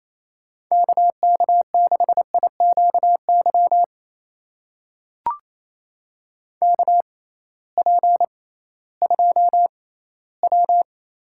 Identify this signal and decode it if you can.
Morse code, heard as a keyed tone, about 28 wpm, spaced as for 16 wpm, 700 Hz. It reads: KK6IQY KP2W
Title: USA Call Signs - 6x Character Spacing - 28wpm